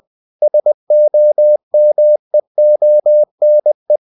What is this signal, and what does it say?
Morse code, heard as a keyed tone, about 20 wpm, 600 Hz. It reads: SOMEONE
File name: cw-books / winnie-the-pooh / winnie-20wpm-page-0107